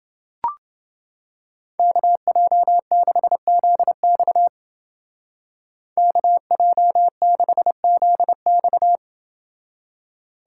Morse code, heard as a keyed tone, about 29 wpm, 700 Hz. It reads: KJ6ZX KJ6ZX